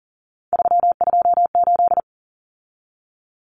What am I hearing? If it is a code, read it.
Morse code, heard as a keyed tone, about 40 wpm, 700 Hz. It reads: 328
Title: Common Cut Numbers - 3-Digit 40wpm